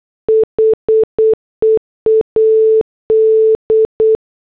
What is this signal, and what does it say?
Morse code, heard as a keyed tone, about 8 wpm, 440 Hz. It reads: HEAD